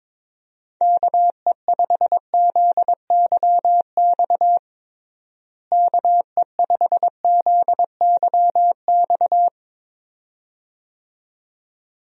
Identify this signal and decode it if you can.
Morse code, heard as a keyed tone, about 22 wpm, 700 Hz. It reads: KE5ZYX KE5ZYX
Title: USA Call Signs - 6 Characters - Repeated 2x - 22wpm